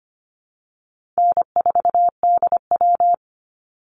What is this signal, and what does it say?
Morse code, heard as a keyed tone, about 25 wpm, 700 Hz. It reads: N4DW
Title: Call Signs - 25wpm